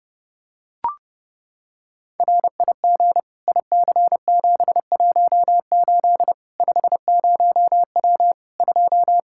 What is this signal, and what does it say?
Morse code, heard as a keyed tone, about 32 wpm, 700 Hz. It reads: RIG IC718 50W 2
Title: K1USN SST QSO Practice 30wpm